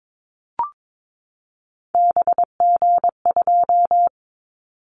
Morse code, heard as a keyed tone, about 22 wpm, 700 Hz. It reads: BG2
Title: Callsign Prefixes for Top 100 DXCC - 22wpm